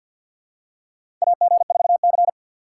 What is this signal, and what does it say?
Morse code, heard as a keyed tone, about 48 wpm, 700 Hz. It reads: AG4C
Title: USA Call Signs - 4 Characters - 2x Character Spacing - 50wpm